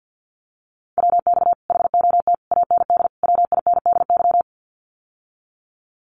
Morse code, heard as a keyed tone, about 48 wpm, 700 Hz. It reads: WX HOT AND WINDY
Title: Common QSO Phrases Encore 50wpm